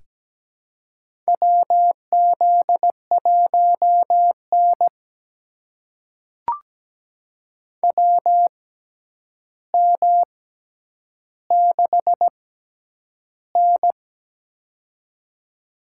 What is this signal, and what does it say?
Morse code, heard as a keyed tone, about 17 wpm, spaced as for 9 wpm, 700 Hz. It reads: WZ1N WM6N
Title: USA Call Signs - 4 Characters - 6x Character Spacing - 17wpm